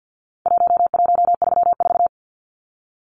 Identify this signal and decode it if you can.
Morse code, heard as a keyed tone, about 50 wpm, 700 Hz. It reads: 1134
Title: Numbers - Contest Style - 4-Digit 50wpm